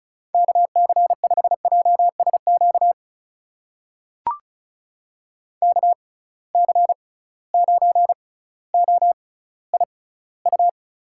Morse code, heard as a keyed tone, about 36 wpm, spaced as for 18 wpm, 700 Hz. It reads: KC5JSQ KC9OIU
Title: USA Call Signs - 6x Character Spacing - 35wpm